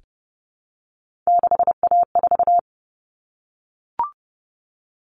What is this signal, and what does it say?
Morse code, heard as a keyed tone, about 30 wpm, 700 Hz. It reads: 6A4
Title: Common Cut Numbers - 3-Digit 30wpm